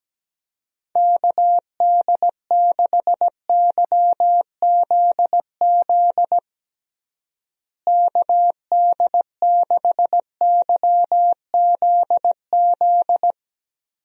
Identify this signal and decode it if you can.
Morse code, heard as a keyed tone, about 17 wpm, 700 Hz. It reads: KD6YZZ KD6YZZ
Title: USA Call Signs - 6 Characters - Repeated 3x - 17wpm